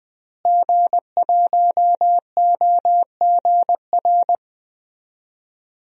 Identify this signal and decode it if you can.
Morse code, heard as a keyed tone, about 20 wpm, 700 Hz. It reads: G1OGR
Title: DXCC-100 Call Signs - 5 Characters - Repeated 2x - 20wpm